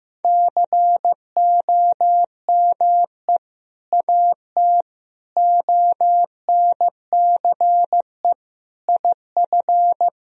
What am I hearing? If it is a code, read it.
Morse code, heard as a keyed tone, about 15 wpm, 700 Hz. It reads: COME AT ONCE IF
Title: Sentences from Top 1000 Words - Review Part 2 - 15wpm